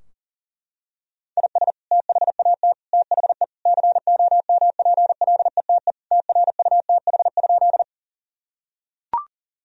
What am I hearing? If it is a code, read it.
Morse code, heard as a keyed tone, about 40 wpm, 700 Hz. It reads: IS THAT THE COMPLETE TRUTH?
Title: Sentences from Top 1000 Words - Review Part 2 - 40wpm